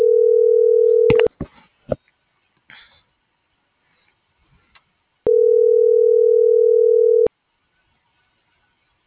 Ringback tone (North America), 440 + 480 Hz.